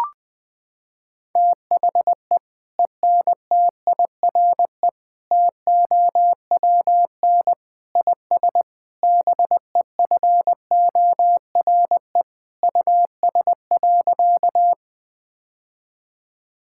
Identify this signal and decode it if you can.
Morse code, heard as a keyed tone, about 20 wpm, 700 Hz. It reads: THE ENTIRE TOWN IS BEFORE US.
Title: Sentences from Top 1000 Words - Review Part 5 - 20wpm